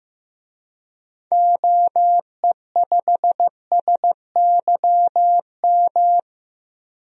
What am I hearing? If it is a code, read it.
Morse code, heard as a keyed tone, about 15 wpm, 700 Hz. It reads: OE5SYM
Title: DXCC-100 Call Signs - 6 Characters - Repeated 2x - 15wpm